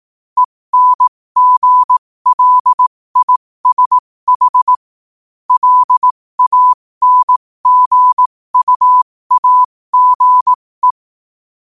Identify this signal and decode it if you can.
Morse code, heard as a keyed tone, about 18 wpm, spaced as for 15 wpm, 1000 Hz. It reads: ENGLISH LANGUAGE
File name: qotdfiles / 15wpm